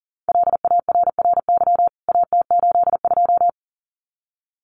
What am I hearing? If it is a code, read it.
Morse code, heard as a keyed tone, about 40 wpm, 700 Hz. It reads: LARRY AT82
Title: CWT Exchange 40wpm